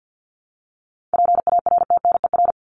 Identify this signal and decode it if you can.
Morse code, heard as a keyed tone, about 50 wpm, 700 Hz. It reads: PARTNER
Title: Top 1501-1750 Words 50wpm